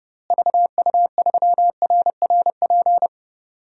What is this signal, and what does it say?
Morse code, heard as a keyed tone, about 32 wpm, 700 Hz. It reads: VU3RRP
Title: DXCC-100 Call Signs - 6 Characters - Repeated 2x - 30wpm